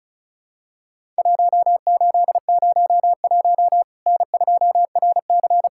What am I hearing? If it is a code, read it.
Morse code, heard as a keyed tone, about 36 wpm, 700 Hz. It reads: 1801 N2RC
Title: Worked All Europe DX Contest - QTC Exchange - 35wpm